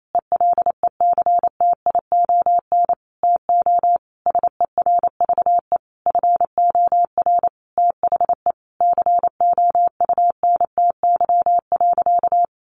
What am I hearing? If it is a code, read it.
Morse code, heard as a keyed tone, about 28 wpm, 700 Hz. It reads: ELECTION TO SERVE FOR THE COUNTY.
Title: Sentences from Top 1500 Words 28wpm